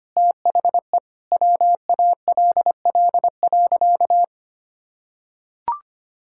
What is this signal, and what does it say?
Morse code, heard as a keyed tone, about 25 wpm, 700 Hz. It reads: THE WALL.
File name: practice / 25wpm / Sentences with 2-Word Phrases - 25wpm